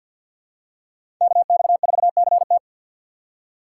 Morse code, heard as a keyed tone, about 47 wpm, 700 Hz. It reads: KX4CT
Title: USA Call Signs - 5 Characters - 3x Character Spacing - 50wpm